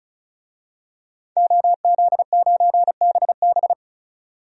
Morse code, heard as a keyed tone, about 36 wpm, 700 Hz. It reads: OZ9BB
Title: DXCC-100 Call Signs - 5 Characters - 35wpm